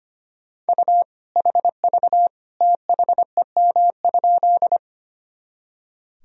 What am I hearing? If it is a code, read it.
Morse code, heard as a keyed tone, about 25 wpm, 700 Hz. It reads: U HV THEM?
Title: Sentences from Top 100 Words and Abbreviations 25wpm